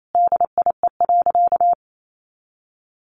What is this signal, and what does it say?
Morse code, heard as a keyed tone, about 28 wpm, 700 Hz. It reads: DIE.
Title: Sentences from Top 1300 Words 28wpm